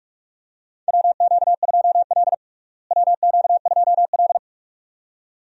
Morse code, heard as a keyed tone, about 46 wpm, 700 Hz. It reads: WQ2L WQ2L
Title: USA Call Signs - 4 Characters - Repeated 2x - 45wpm